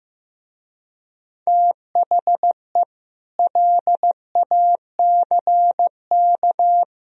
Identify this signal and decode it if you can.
Morse code, heard as a keyed tone, about 15 wpm, 700 Hz. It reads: THE LACK
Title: Speed-Racing - Sets of 3 Words 15wpm